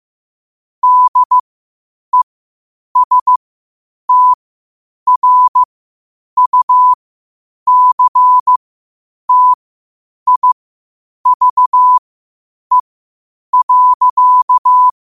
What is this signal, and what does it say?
Morse code, heard as a keyed tone, about 15 wpm, spaced as for 8 wpm, 1000 Hz. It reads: DESTRUCTIVE.
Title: 8 WPM morse code quotes for Fri, 01 Mar 2024